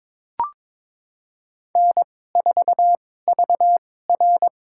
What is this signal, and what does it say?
Morse code, heard as a keyed tone, about 23 wpm, spaced as for 16 wpm, 700 Hz. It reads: N4VR
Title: USA Call Signs - 4 Characters - 2x Character Spacing - 22wpm